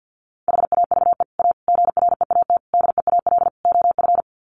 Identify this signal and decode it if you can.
Morse code, heard as a keyed tone, about 50 wpm, 700 Hz. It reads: HAVE A GREAT DEAL OF
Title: Sets of 5 Words Encore2 50wpm